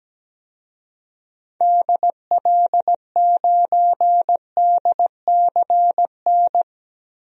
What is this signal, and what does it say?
Morse code, heard as a keyed tone, about 17 wpm, 700 Hz. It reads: DL9DCN
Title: DXCC-100 Call Signs - 6 Characters - Repeated 3x - 17wpm